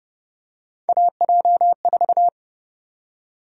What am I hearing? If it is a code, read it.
Morse code, heard as a keyed tone, about 30 wpm, 700 Hz. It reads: AJ4
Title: USA Call Sign Prefixes - Repeated 2x - 30wpm